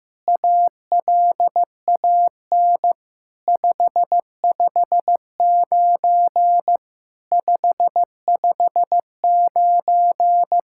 Morse code, heard as a keyed tone, about 15 wpm, 700 Hz. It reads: ALAN 559 559
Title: SOTA QSO Practice 15wpm